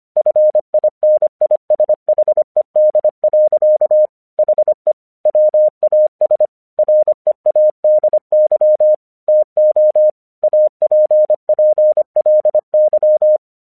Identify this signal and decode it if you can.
Morse code, heard as a keyed tone, about 26 wpm, 610 Hz. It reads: FINISHED. HE WAS READY TO APPLY
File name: 1 English 25wpm / AQC27